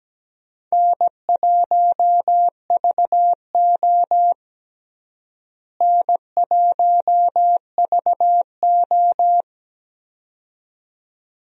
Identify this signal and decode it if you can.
Morse code, heard as a keyed tone, about 17 wpm, 700 Hz. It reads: N1VO N1VO
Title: USA Call Signs - 4 Characters - Repeated 2x - 17wpm